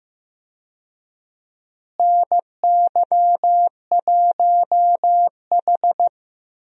Morse code, heard as a keyed tone, about 15 wpm, 700 Hz. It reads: NY1H
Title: USA Call Signs - 4 Characters - Repeated 2x - 15wpm